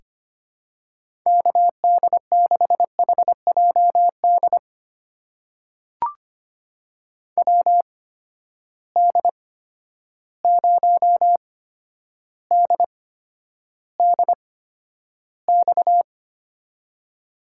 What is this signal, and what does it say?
Morse code, heard as a keyed tone, about 25 wpm, spaced as for 11 wpm, 700 Hz. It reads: KD6HJD WD0DDX
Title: USA Call Signs - 6 Characters - 8x Character Spacing - 25wpm